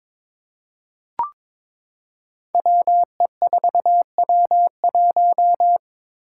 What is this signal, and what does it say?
Morse code, heard as a keyed tone, about 22 wpm, 700 Hz. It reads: WE4W1
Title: Five Challenging Letter-Number Combinations 22wpm